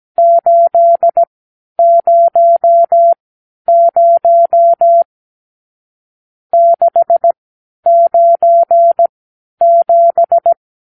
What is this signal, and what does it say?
Morse code, heard as a keyed tone, about 17 wpm, spaced as for 10 wpm, 680 Hz. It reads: 800 697
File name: arrl / Review (1-0)